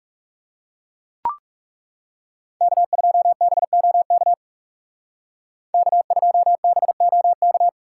Morse code, heard as a keyed tone, about 37 wpm, 700 Hz. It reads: K2BOK K2BOK
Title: Speed Racing - USA 1x3 Call Signs 30wpm